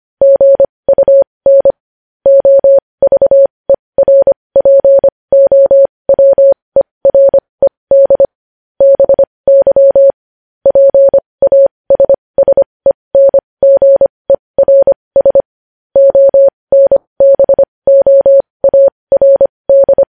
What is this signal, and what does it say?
Morse code, heard as a keyed tone, about 25 wpm, spaced as for 20 wpm, 550 Hz. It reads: GUN OVERPOWERED BY PASSENGERS ONBOARD
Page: News Headlines in Morse Code at 20 WPM